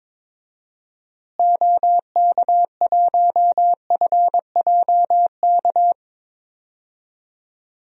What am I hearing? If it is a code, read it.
Morse code, heard as a keyed tone, about 22 wpm, 700 Hz. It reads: OK1FJK
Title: DXCC-100 Call Signs - 6 Characters - Repeated 3x - 22wpm